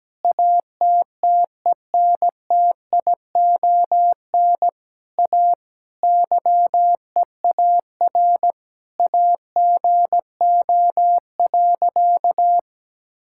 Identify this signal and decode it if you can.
Morse code, heard as a keyed tone, about 17 wpm, 700 Hz. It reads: ATTENTION A YEAR AGO.
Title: Sentences from Top 1000 Words - Review Part 6 - 17wpm